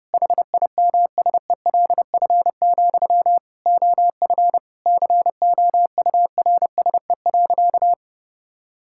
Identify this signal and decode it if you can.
Morse code, heard as a keyed tone, about 30 wpm, 700 Hz. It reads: HIMSELF, OF COURSE.
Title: Sentences from Top 1600 Words 30wpm